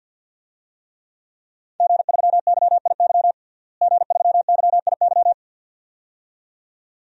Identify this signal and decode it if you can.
Morse code, heard as a keyed tone, about 45 wpm, 700 Hz. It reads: G3YIY G3YIY
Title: DXCC-100 Call Signs - 5 Characters - Repeated 2x - 50wpm